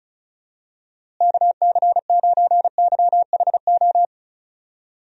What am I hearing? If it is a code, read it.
Morse code, heard as a keyed tone, about 34 wpm, 700 Hz. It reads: KC9YHO
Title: USA Call Signs - 6 Characters - Repeated 2x - 35wpm